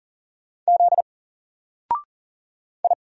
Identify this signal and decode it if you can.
Morse code, heard as a keyed tone, about 39 wpm, 700 Hz. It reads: Z I
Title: Single Character (Letter:Number:Punctuation:Prosign) 40wpm